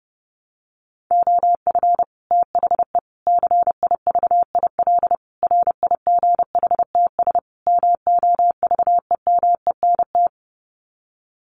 Morse code, heard as a keyed tone, about 30 wpm, 700 Hz. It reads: OF THE CIVIL RIGHTS MOVEMENT
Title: Sets of 5 Words Encore2 30wpm